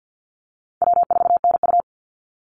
Morse code, heard as a keyed tone, about 48 wpm, 700 Hz. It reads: W4NU
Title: Easy Call Signs - 50wpm